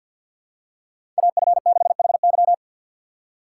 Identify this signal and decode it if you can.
Morse code, heard as a keyed tone, about 52 wpm, 700 Hz. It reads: AF6HY
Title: USA Call Signs - 4x Character Spacing - 50wpm